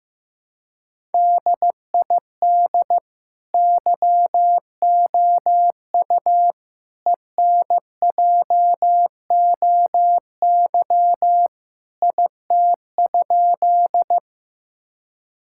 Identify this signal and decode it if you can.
Morse code, heard as a keyed tone, about 16 wpm, 700 Hz. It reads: DID YOU ENJOY IT?
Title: Sentences from Top 1000 Words - Review Part 5 - 15wpm